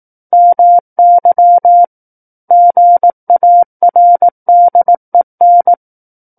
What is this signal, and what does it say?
Morse code, heard as a keyed tone, about 18 wpm, 700 Hz. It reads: MY GARDEN
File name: 18 WPM